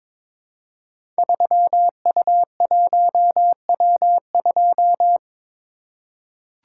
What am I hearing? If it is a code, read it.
Morse code, heard as a keyed tone, about 22 wpm, 700 Hz. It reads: 3U1W2
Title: Code Groups - 5 Letter:Numbers - 22wpm